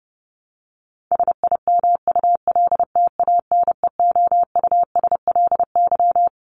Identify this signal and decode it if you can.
Morse code, heard as a keyed tone, about 30 wpm, 700 Hz. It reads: SIMULTANEOUSLY
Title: LY Suffix - Top 100 Words with Base Words 30wpm